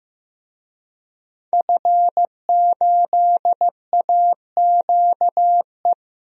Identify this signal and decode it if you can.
Morse code, heard as a keyed tone, about 15 wpm, 700 Hz. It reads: F8AQE
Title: DXCC-100 Call Signs - 5 Characters - 15wpm